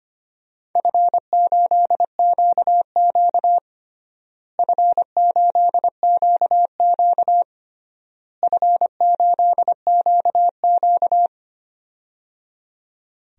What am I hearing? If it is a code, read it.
Morse code, heard as a keyed tone, about 25 wpm, 700 Hz. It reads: F8QQ F8QQ F8QQ
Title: DXCC-100 Call Signs - 4 Characters - Repeated 3x - 25wpm